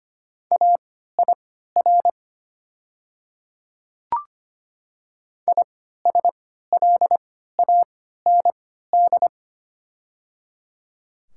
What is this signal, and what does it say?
Morse code, heard as a keyed tone, about 25 wpm, spaced as for 14 wpm, 700 Hz. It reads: AIR ISLAND
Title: 3x Character Spacing - Top 500 Words - 25wpm